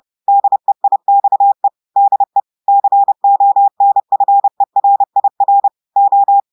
Morse code, heard as a keyed tone, about 30 wpm, 800 Hz. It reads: DEIXE DE CONFERIR O
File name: portugais texte en clair 2 vitesse 30 mots minutes